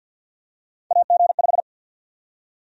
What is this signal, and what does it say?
Morse code, heard as a keyed tone, about 53 wpm, 700 Hz. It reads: AG5
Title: USA Call Sign Prefixes - 50wpm